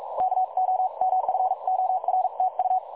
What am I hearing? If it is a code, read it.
Morse code, heard as a keyed tone, about 42 wpm, spaced as for 37 wpm, 750 Hz. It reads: KOMFORTU
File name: dlazaawansowanych / nagraniazpasma / websdr recording 3565.0kHz